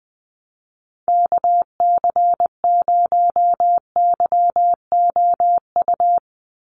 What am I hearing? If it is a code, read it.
Morse code, heard as a keyed tone, about 20 wpm, 700 Hz. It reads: KC0YOU
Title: USA Call Signs 20wpm